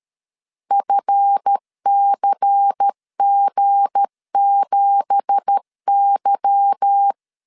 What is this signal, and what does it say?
Morse code, heard as a keyed tone, about 13 wpm, 800 Hz. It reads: FCG7Y
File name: LEZIONI / 9 2867 12wpm 2322